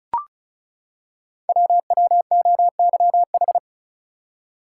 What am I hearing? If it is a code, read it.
Morse code, heard as a keyed tone, about 36 wpm, 690 Hz. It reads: WWOYH
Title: Code Groups - 5 Letters - 35wpm